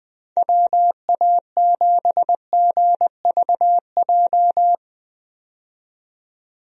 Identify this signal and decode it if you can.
Morse code, heard as a keyed tone, about 20 wpm, 700 Hz. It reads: WA7GVJ
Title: USA Call Signs - 6 Characters - 20wpm